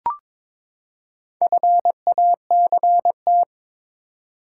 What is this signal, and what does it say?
Morse code, heard as a keyed tone, about 21 wpm, 700 Hz. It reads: FACT
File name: practice / 22wpm / Build A Phrase - 5-Word Phrases - 22wpm